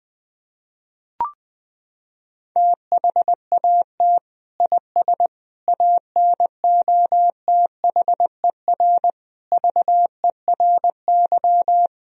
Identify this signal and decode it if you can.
Morse code, heard as a keyed tone, about 20 wpm, 700 Hz. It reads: THAT IS ANOTHER VERY
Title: Sentences from Top 1000 Words - Review Part 2 - 20wpm